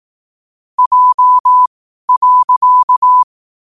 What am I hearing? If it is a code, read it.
Morse code, heard as a keyed tone, about 18 wpm, spaced as for 12 wpm, 1000 Hz. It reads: J.
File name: qotdfiles / 12wpm